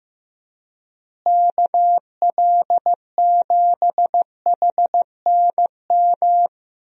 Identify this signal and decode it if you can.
Morse code, heard as a keyed tone, about 15 wpm, 700 Hz. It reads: KL7HNM
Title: USA Call Signs - 6 Characters - Repeated 3x - 15wpm